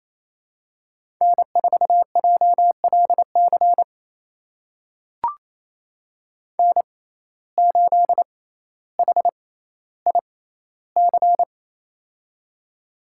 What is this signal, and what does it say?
Morse code, heard as a keyed tone, about 27 wpm, spaced as for 14 wpm, 700 Hz. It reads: N4JLC N8HIC
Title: USA Call Signs - 6x Character Spacing - 28wpm